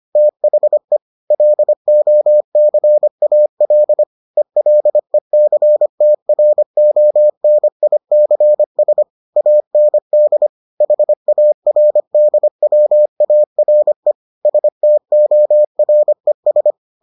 Morse code, heard as a keyed tone, about 25 wpm, 600 Hz. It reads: THE LOCAL ELECTRONICS AND HARDWARE STORES